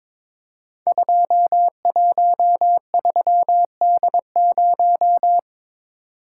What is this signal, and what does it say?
Morse code, heard as a keyed tone, about 22 wpm, 700 Hz. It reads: 213D0
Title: Code Groups - 5 Letter:Numbers - 22wpm